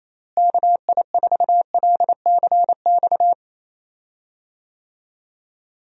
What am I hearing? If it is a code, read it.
Morse code, heard as a keyed tone, about 27 wpm, 700 Hz. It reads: KI4LCX
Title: USA Call Signs - 6 Characters - Repeated 2x - 28wpm